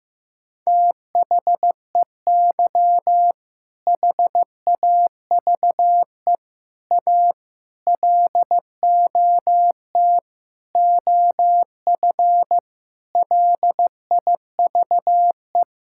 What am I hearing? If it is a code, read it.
Morse code, heard as a keyed tone, about 15 wpm, 700 Hz. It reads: THEY HAVE A LOT OF LIVE
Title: Sentences from Top 1000 Words - Review Part 5 - 15wpm